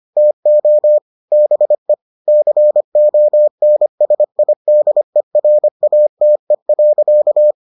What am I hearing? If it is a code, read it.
Morse code, heard as a keyed tone, about 25 wpm, 600 Hz. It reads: TO BE CONSIDERATE.